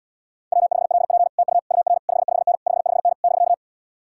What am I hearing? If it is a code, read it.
Morse code, heard as a keyed tone, about 75 wpm, 700 Hz. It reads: W5LP TU UR 55N 55N <BK>